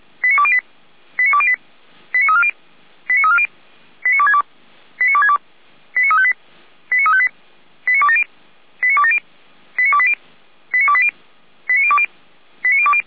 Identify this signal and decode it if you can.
Five-tone selective call (ZVEI), heard as quick runs of five tones, tones of about 70 ms. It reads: 89208 89208 89340 89340 89262 89262 89358 89358 89270 89270 89280 89280 89020 89020